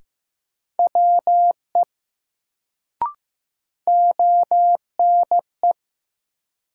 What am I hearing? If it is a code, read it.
Morse code, heard as a keyed tone, about 15 wpm, 700 Hz. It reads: WE ONE
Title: Sentences from Top 100 Words - New Word Drill - 15wpm